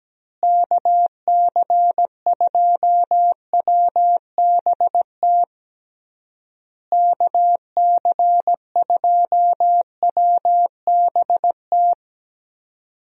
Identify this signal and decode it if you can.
Morse code, heard as a keyed tone, about 17 wpm, 700 Hz. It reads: KC2WBT KC2WBT
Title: USA Call Signs - 6 Characters - Repeated 3x - 17wpm